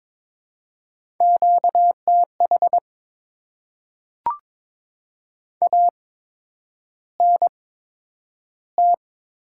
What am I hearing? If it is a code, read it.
Morse code, heard as a keyed tone, about 22 wpm, spaced as for 9 wpm, 700 Hz. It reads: QTH ANT
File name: Top 20 QSO Elements - 8x Character Spacing - 22wpm